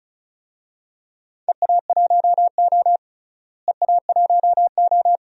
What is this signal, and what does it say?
Morse code, heard as a keyed tone, about 36 wpm, 700 Hz. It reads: EA1O EA1O
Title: DXCC-100 Call Signs - 4 Characters - Repeated 2x - 35wpm